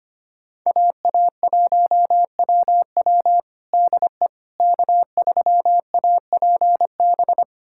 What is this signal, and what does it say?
Morse code, heard as a keyed tone, about 25 wpm, 700 Hz. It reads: AA1WW DE K3APB
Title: USA Callsign DE Callsign - 5 Characters - 25wpm